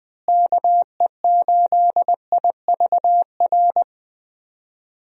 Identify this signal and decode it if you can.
Morse code, heard as a keyed tone, about 20 wpm, 700 Hz. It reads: KE8IVR
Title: USA Call Signs - 6 Characters - Repeated 2x - 20wpm